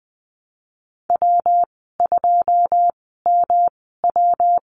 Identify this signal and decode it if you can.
Morse code, heard as a keyed tone, about 20 wpm, spaced as for 15 wpm, 700 Hz. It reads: W2MW
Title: 179 Lesson 29g - Slash - Call Signs 20wpm